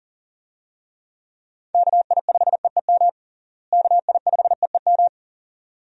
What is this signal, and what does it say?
Morse code, heard as a keyed tone, about 40 wpm, 700 Hz. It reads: KI5EEM KI5EEM
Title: USA Call Signs - 6 Characters - Repeated 2x - 40wpm